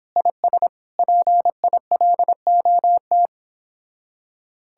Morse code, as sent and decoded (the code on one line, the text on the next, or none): .. ... / .--. .. .-.. --- -
IS PILOT